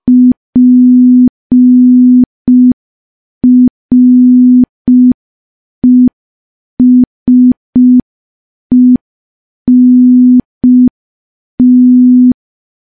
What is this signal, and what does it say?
Morse code, heard as a keyed tone, about 5 wpm, 250 Hz. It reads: PRESENT
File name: morse / morse-training-05WPM